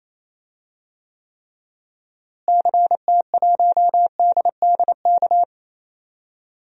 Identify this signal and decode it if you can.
Morse code, heard as a keyed tone, about 28 wpm, 700 Hz. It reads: CT1DDK
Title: DXCC-100 Call Signs - 6 Characters - Repeated 3x - 28wpm